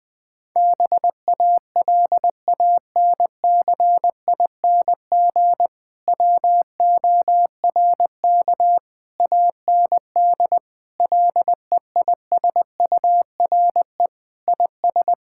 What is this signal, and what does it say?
Morse code, heard as a keyed tone, about 20 wpm, 700 Hz. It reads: BALANCING WORK AND LEISURE IS
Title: Difficult Sentences with Hints - 20wpm